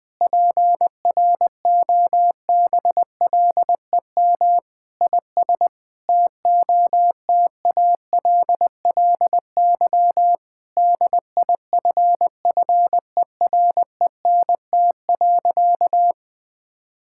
Morse code, heard as a keyed tone, about 21 wpm, 700 Hz. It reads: PROBLEM IS TOTALLY DIFFERENT.
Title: Sentences from Top 1600 Words 20wpm